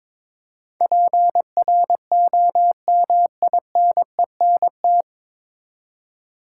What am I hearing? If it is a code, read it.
Morse code, heard as a keyed tone, about 22 wpm, 700 Hz. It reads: PROMINENT
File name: 9 Letter Words - First 3 Letters Given - 22wpm